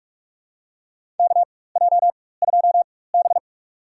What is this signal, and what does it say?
Morse code, heard as a keyed tone, about 44 wpm, spaced as for 21 wpm, 700 Hz. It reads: KJ2B
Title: USA Call Signs - 4 Characters - 4x Character Spacing - 45wpm